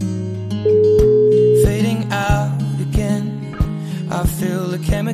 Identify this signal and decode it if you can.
Ringback tone (Europe, 425 Hz), with noise.